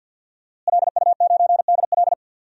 Morse code, heard as a keyed tone, about 48 wpm, 700 Hz. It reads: LU9DL
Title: DXCC-100 Call Signs - 5 Characters - Repeated 2x - 50wpm